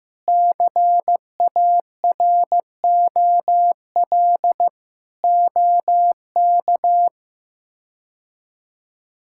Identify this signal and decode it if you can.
Morse code, heard as a keyed tone, about 15 wpm, 700 Hz. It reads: CAROL OK
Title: K1USN SST QSO Practice - 15wpm